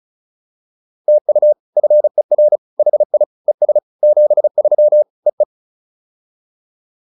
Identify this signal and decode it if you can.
Morse code, heard as a keyed tone, about 34 wpm, 600 Hz. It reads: TU FER HI ES 73 EE